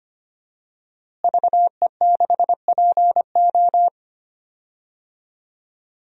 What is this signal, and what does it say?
Morse code, heard as a keyed tone, about 25 wpm, 700 Hz. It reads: VE6PO
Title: DXCC-100 Call Signs - 5 Characters - Repeated 3x - 25wpm